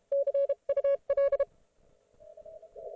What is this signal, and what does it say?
Morse code, heard as a keyed tone, about 32 wpm, spaced as for 28 wpm, 560 Hz. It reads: CUL